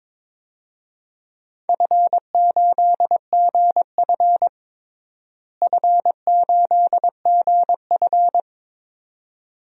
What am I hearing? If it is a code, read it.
Morse code, heard as a keyed tone, about 22 wpm, 700 Hz. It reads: F8GF F8GF
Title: DXCC-100 Call Signs - 4 Characters - Repeated 2x - 22wpm